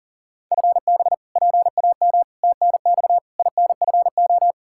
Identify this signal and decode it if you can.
Morse code, heard as a keyed tone, about 40 wpm, 700 Hz. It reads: FB PAM TNX INFO
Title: K1USN SST QSO Practice 40wpm